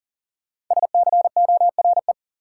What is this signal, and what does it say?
Morse code, heard as a keyed tone, about 40 wpm, 700 Hz. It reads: SCORE
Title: Sentences from Top 1600 Words - New Word Drill Alt - 40wpm